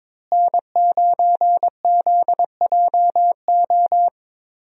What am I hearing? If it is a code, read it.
Morse code, heard as a keyed tone, about 22 wpm, 700 Hz. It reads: N9ZJO
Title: USA Call Signs - 5 Characters - Repeated 3x - 22wpm